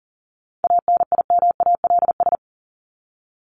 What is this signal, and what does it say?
Morse code, heard as a keyed tone, about 40 wpm, 700 Hz. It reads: ANIMALS
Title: ES:S Suffix - Top 100 Words with Base Words 40wpm